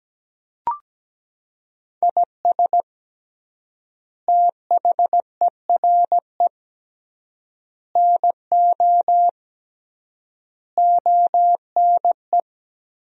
Morse code, heard as a keyed tone, about 17 wpm, 700 Hz. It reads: IS THERE NO ONE
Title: 3x Word-Spacing - Sentences from Top 200 Words - 17wpm